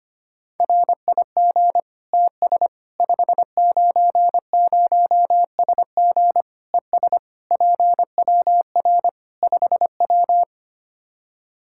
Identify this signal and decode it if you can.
Morse code, heard as a keyed tone, about 25 wpm, 700 Hz. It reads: RIG TS 590SG ES PWR 5W
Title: Practice QSO - Exchange 2 - Info - 25wpm